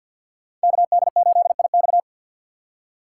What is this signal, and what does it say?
Morse code, heard as a keyed tone, about 52 wpm, 690 Hz. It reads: KD8IX
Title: USA Call Signs - 5 Characters - 2x Character Spacing - 50wpm